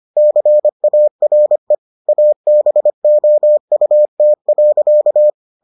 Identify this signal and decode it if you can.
Morse code, heard as a keyed tone, about 25 wpm, 600 Hz. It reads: CARE ABOUT.